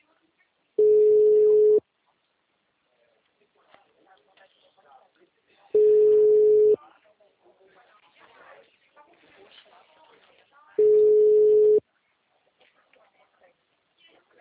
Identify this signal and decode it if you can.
Ringback tone (Europe, 425 Hz), with speech.
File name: out- -2002- - - .58251